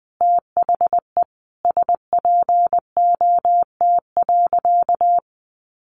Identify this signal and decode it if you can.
Morse code, heard as a keyed tone, about 20 wpm, 700 Hz. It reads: THE SPOT.
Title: Sentences from Top 1100 Words 20wpm